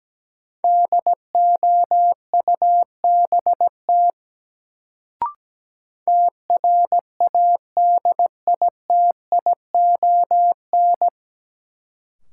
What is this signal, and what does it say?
Morse code, heard as a keyed tone, about 17 wpm, 700 Hz. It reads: DOUBT TRADITION
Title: Sentences from Top 1500 Words - New Word Drill - 17wpm